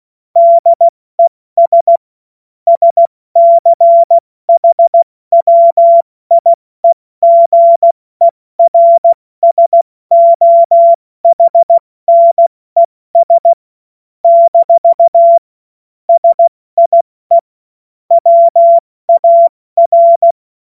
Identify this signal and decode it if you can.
Morse code, heard as a keyed tone, about 16 wpm, 680 Hz. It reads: DES SCHWIEGERSOHNES - SIE WAR
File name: Morsekurs / Witze / Witz 25. Januar 2022, 70 F80